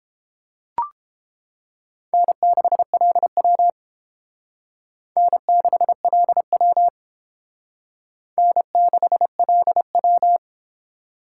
Morse code, heard as a keyed tone, about 30 wpm, 700 Hz. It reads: N6LW N6LW N6LW